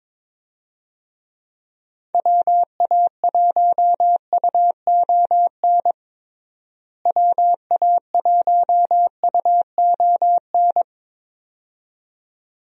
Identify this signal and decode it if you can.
Morse code, heard as a keyed tone, about 22 wpm, 700 Hz. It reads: WA1UON WA1UON
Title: USA Call Signs - 6 Characters - Repeated 2x - 22wpm